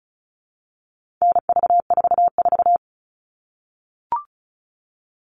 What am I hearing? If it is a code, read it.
Morse code, heard as a keyed tone, about 34 wpm, 700 Hz. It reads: NV44
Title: All Cut Numbers - 4-Digit 35wpm